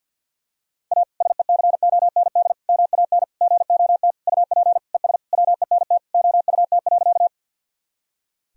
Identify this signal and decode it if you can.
Morse code, heard as a keyed tone, about 48 wpm, 700 Hz. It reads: A SECOND MAN GOT UP ES WENT OUT.
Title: Sentences from Top 300 Words and Abbreviations 50wpm